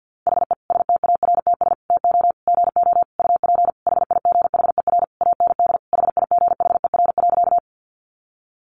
Morse code, heard as a keyed tone, about 50 wpm, 700 Hz. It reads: HE STARTS TO GO UP HIGHER AND HIGHER.
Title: Additional Easy Phrases 50wpm